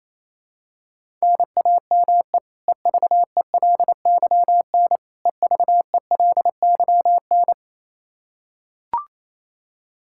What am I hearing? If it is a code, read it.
Morse code, heard as a keyed tone, about 28 wpm, 700 Hz. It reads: NAME EVELYN EVELYN
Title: Practice QSO - Exchange 1 - Beginning - 28wpm